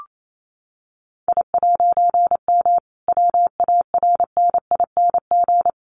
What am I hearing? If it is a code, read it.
Morse code, heard as a keyed tone, about 28 wpm, 700 Hz. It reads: I'M WARNING
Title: A Sound Of Thunder 28wpm